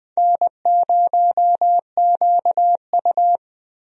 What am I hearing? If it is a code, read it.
Morse code, heard as a keyed tone, about 20 wpm, 700 Hz. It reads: N0QU
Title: USA Call Signs - 4 Characters - Repeated 2x - 20wpm